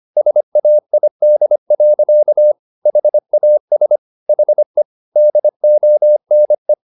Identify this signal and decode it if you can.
Morse code, heard as a keyed tone, about 25 wpm, 600 Hz. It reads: SAID. HAS HE DONE
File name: the old man and the sea part 1 25wpm